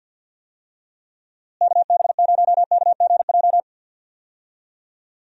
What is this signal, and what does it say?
Morse code, heard as a keyed tone, about 48 wpm, 700 Hz. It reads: KB0KGJ
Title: USA Call Signs - 6 Characters - Repeated 3x - 50wpm